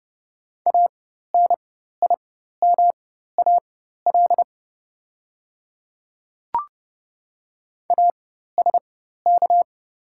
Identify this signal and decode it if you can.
Morse code, heard as a keyed tone, about 30 wpm, spaced as for 14 wpm, 700 Hz. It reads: ANIMAL ASK
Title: 4x Character Spacing - Top 500 Words - 30wpm